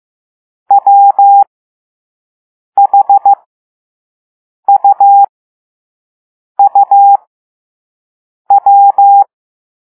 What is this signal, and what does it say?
Morse code, heard as a keyed tone, about 15 wpm, spaced as for 5 wpm, 800 Hz. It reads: WHUUW